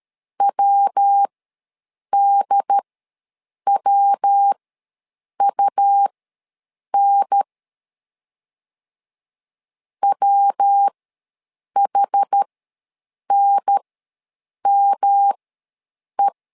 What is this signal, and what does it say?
Morse code, heard as a keyed tone, about 13 wpm, spaced as for 7 wpm, 800 Hz. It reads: WDWUN WHNME
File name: LEZIONI / 3 MWRO / 3 MW ws 12wpm 1816